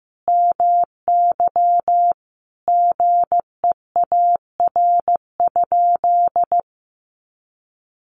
Morse code, heard as a keyed tone, about 15 wpm, 700 Hz. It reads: MY GEAR?
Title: Sentences from Top 1400 Words 15wpm